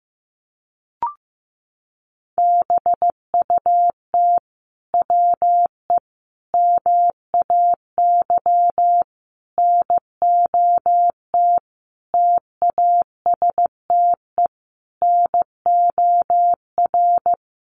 Morse code, heard as a keyed tone, about 15 wpm, 700 Hz. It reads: BUT WE MAY NOT TASTE NOR
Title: Sentences from Top 1500 Words 15wpm